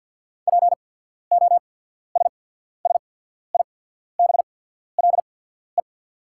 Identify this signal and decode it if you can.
Morse code, heard as a keyed tone, about 48 wpm, spaced as for 14 wpm, 690 Hz. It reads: POSSIBLE
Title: 8x Character Spacing - Top 500 Words - 50wpm